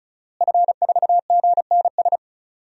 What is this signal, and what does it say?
Morse code, heard as a keyed tone, about 36 wpm, 700 Hz. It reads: F4GNS
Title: DXCC-100 Call Signs - 5 Characters - Repeated 3x - 35wpm